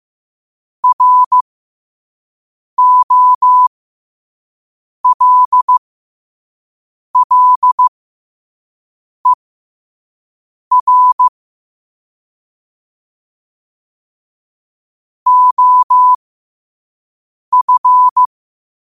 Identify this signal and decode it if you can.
Morse code, heard as a keyed tone, about 15 wpm, spaced as for 5 wpm, 1000 Hz. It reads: ROLLER OF
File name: qotdfiles / 5wpm